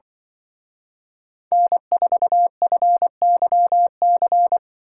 Morse code, heard as a keyed tone, about 24 wpm, 700 Hz. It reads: N4FYC
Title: Speed Racing - USA 1x3 Call Signs 20wpm